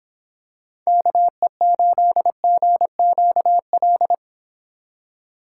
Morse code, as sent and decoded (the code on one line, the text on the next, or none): -.- . ---.. --. --.- .-..
KE8GQL